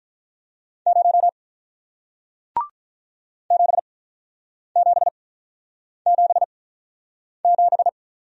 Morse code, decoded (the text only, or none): 0 7777